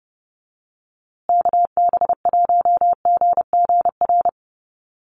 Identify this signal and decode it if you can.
Morse code, heard as a keyed tone, about 30 wpm, 700 Hz. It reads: KB1GGR
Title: USA Call Signs Repeated 3x 30wpm